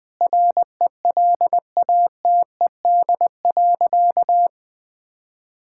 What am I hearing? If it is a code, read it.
Morse code, heard as a keyed tone, about 20 wpm, 700 Hz. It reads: RELATED.
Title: Sentences from Top 1000 Words - Review Part 5 - 20wpm